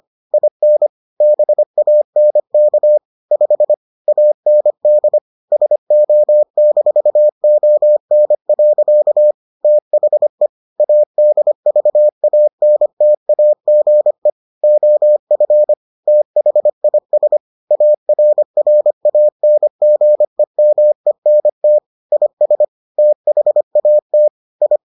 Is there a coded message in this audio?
Yes: Morse code from 0.3 to 24.8 s, heard as a keyed tone, about 25 wpm, 600 Hz. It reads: IN BANK 5 AND SO-ON. THE ADVANTAGE OF THIS ARRANGEMENT IS THAT I